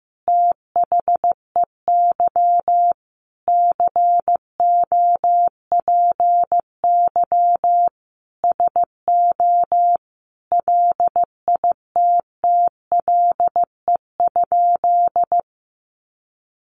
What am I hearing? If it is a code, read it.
Morse code, heard as a keyed tone, about 15 wpm, 700 Hz. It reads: THEY COPY SO LITTLE?
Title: Sentences from Top 1400 Words 15wpm